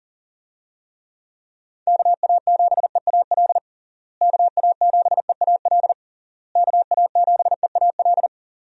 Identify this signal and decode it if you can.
Morse code, heard as a keyed tone, about 40 wpm, 700 Hz. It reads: KA7EAL KA7EAL KA7EAL
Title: USA Call Signs - 6 Characters - Repeated 3x - 40wpm